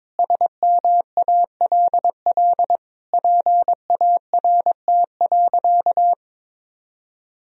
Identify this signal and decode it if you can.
Morse code, heard as a keyed tone, about 22 wpm, 700 Hz. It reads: SMALL PART.
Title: Sentences from Top 1000 Words - Review Part 1 - 22wpm